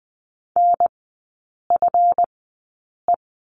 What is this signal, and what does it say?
Morse code, heard as a keyed tone, about 20 wpm, spaced as for 8 wpm, 690 Hz. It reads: NFE
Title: 145 Lesson 24e - Letter y - Three Characters 20wpm